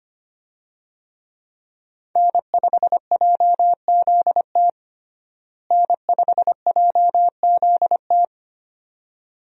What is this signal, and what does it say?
Morse code, heard as a keyed tone, about 25 wpm, 700 Hz. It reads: N5JZT N5JZT
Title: USA Call Signs - 5 Characters - Repeated 2x - 25wpm